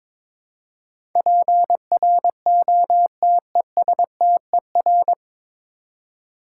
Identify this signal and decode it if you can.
Morse code, heard as a keyed tone, about 22 wpm, 700 Hz. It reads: PROTESTER
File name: 9 Letter Words - First 3 Letters Given - 22wpm